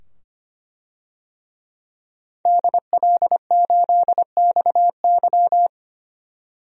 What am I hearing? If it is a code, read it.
Morse code, heard as a keyed tone, about 25 wpm, 700 Hz. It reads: DL8XY
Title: DXCC-100 Call Signs - 5 Characters - 25wpm